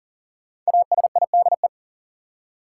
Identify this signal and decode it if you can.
Morse code, heard as a keyed tone, about 40 wpm, 700 Hz. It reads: ASIDE
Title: Five Letter Words - 40wpm